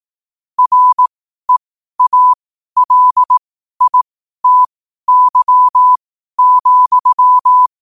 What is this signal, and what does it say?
Morse code, heard as a keyed tone, about 18 wpm, spaced as for 12 wpm, 1000 Hz. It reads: REALITY,